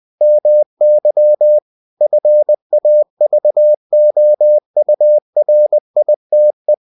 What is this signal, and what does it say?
Morse code, heard as a keyed tone, about 20 wpm, 600 Hz. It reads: MY FAVOURITE